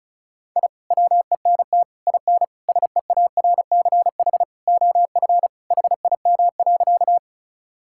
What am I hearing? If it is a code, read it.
Morse code, heard as a keyed tone, about 36 wpm, 700 Hz. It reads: I WENT IN SEARCH OF HIM.
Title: Sentences from Top 1000 Words - Review Part 5 - 35wpm